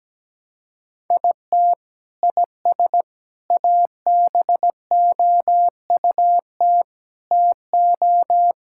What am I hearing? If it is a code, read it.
Morse code, heard as a keyed tone, about 17 wpm, 700 Hz. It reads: IT IS ABOUT TO
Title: Sentences from Top 1600 Words 17wpm